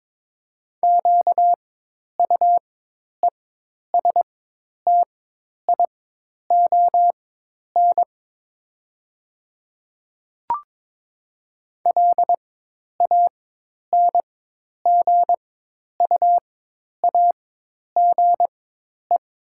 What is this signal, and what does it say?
Morse code, heard as a keyed tone, about 22 wpm, spaced as for 10 wpm, 700 Hz. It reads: QUESTION LANGUAGE
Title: 4x Character Spacing - Top 500 Words - 22wpm